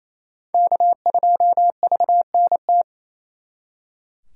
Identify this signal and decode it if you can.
Morse code, heard as a keyed tone, about 35 wpm, 700 Hz. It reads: K2VNT